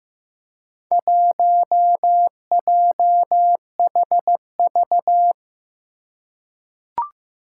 Morse code, heard as a keyed tone, about 15 wpm, 700 Hz. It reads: 1JHV
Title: Four Challenging Letter-Number Combinations 15wpm